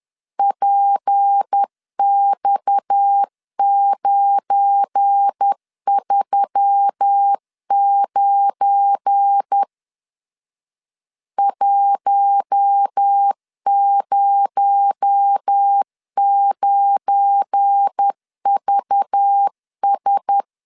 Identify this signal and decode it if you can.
Morse code, heard as a keyed tone, about 10 wpm, 800 Hz. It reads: PX939 109VS
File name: LEZIONI / 7 GF9Z / 7 GF9 10wpm 2747